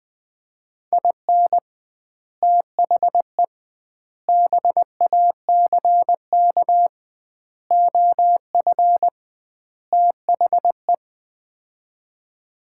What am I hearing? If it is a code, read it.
Morse code, heard as a keyed tone, about 20 wpm, 700 Hz. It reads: IN THE BACK OF THE
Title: 2x Word-Spacing - Sets of 5 Words - 20wpm